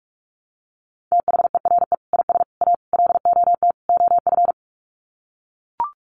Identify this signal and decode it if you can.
Morse code, heard as a keyed tone, about 44 wpm, 700 Hz. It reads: THERE IS A LOT OF
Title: Sets of 5 Words 45wpm